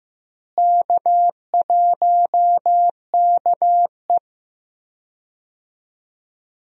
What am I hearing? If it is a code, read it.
Morse code, heard as a keyed tone, about 15 wpm, 700 Hz. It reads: K1KE
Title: USA Call Signs - 4 Characters - Repeated 3x - 15wpm